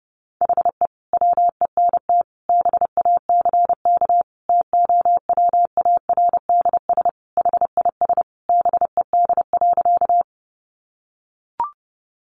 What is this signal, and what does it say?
Morse code, heard as a keyed tone, about 30 wpm, 700 Hz. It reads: HE WENT BACK TOWARDS HIS BED.
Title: Sentences from Top 1100 Words 30wpm